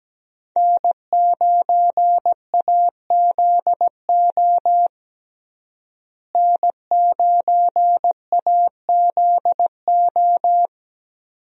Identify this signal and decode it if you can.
Morse code, heard as a keyed tone, about 17 wpm, 700 Hz. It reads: N9AZO N9AZO
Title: USA Call Signs - 5 Characters - Repeated 2x - 17wpm